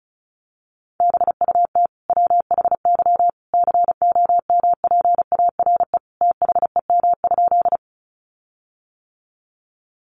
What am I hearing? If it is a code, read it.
Morse code, heard as a keyed tone, about 34 wpm, 700 Hz. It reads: BUT WHY COMPARE THEM?
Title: Sentences from Top 1300 Words 35wpm